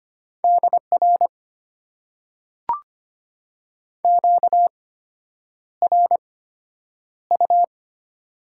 Morse code, heard as a keyed tone, about 25 wpm, spaced as for 9 wpm, 690 Hz. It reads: DR QRU
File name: Top 90 QSO Elements - 8x Character Spacing - 25wpm